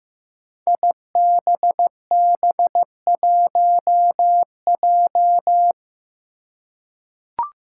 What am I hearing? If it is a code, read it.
Morse code, heard as a keyed tone, about 15 wpm, 700 Hz. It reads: IBB1J
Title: Five Challenging Letter-Number Combinations 15wpm